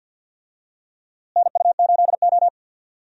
Morse code, heard as a keyed tone, about 46 wpm, 700 Hz. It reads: NU8O
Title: USA Call Signs - 4 Characters - Repeated 3x - 50wpm